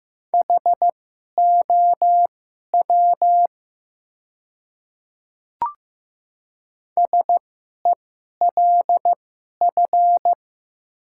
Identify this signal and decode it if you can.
Morse code, heard as a keyed tone, about 15 wpm, spaced as for 11 wpm, 700 Hz. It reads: HOW SELF
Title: 2x Character Spacing - Top 200 Words - 15wpm